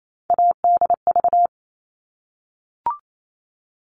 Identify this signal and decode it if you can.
Morse code, heard as a keyed tone, about 28 wpm, 700 Hz. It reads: ADV
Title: All Cut Numbers - 3-Digit 28wpm